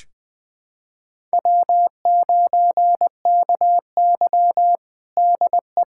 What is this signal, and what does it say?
Morse code, heard as a keyed tone, about 20 wpm, 700 Hz. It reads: W9KY DE
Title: USA Callsign DE Callsign - 4 Characters - 20wpm